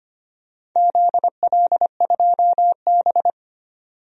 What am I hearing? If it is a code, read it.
Morse code, heard as a keyed tone, about 25 wpm, 700 Hz. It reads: ZL2B
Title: DXCC-100 Call Signs - 4 Characters - 25wpm